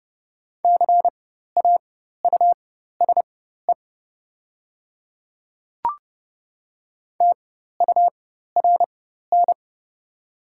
Morse code, heard as a keyed tone, about 30 wpm, spaced as for 14 wpm, 700 Hz. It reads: CAUSE TURN
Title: 4x Character Spacing - Top 200 Words - 30wpm